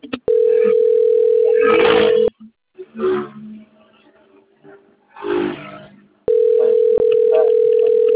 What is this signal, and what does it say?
Ringback tone (North America), 440 + 480 Hz.